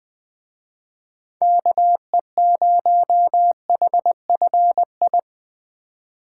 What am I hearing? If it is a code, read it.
Morse code, heard as a keyed tone, about 21 wpm, 700 Hz. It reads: KE0HFI